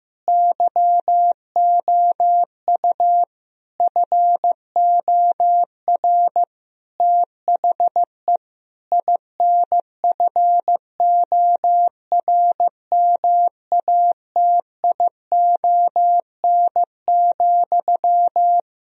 Morse code, heard as a keyed tone, about 15 wpm, 700 Hz. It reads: YOU FOR THE INFORMATION,
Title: Sentences from Top 1000 Words - Review Part 3 - 15wpm